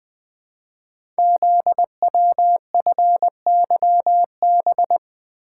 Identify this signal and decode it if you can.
Morse code, heard as a keyed tone, about 20 wpm, 700 Hz. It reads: ZWFYB
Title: Code Groups - 5 Letters - 20wpm